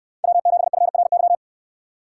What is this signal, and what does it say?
Morse code, heard as a keyed tone, about 66 wpm, 700 Hz. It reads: W7FRY